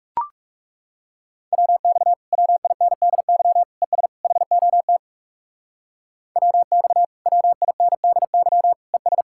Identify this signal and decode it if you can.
Morse code, heard as a keyed tone, about 37 wpm, 700 Hz. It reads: WX WINDY ES HOT WX WINDY ES
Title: Speed-Racing Common QSO Phrases - 30wpm